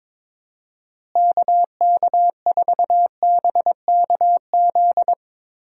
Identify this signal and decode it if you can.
Morse code, heard as a keyed tone, about 22 wpm, 700 Hz. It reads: KK4BKZ
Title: USA Call Signs - 6 Characters - Repeated 3x - 22wpm